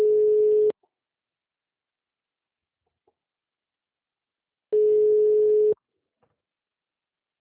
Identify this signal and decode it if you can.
Ringback tone (Europe, 425 Hz).
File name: out- -2005- - - .4104